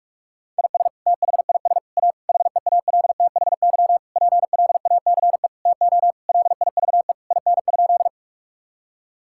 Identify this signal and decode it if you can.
Morse code, heard as a keyed tone, about 47 wpm, 700 Hz. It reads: IS THIS A HEALTHY PLACE TO LIVE IN?
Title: Sentences from Top 1000 Words - Review Part 5 - 45wpm